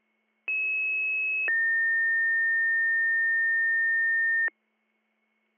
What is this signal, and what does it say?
Two-tone page, heard as a one-second tone, then a three-second tone, about 2600 then 1800 Hz. It is heard from 0.5 to 4.5 s.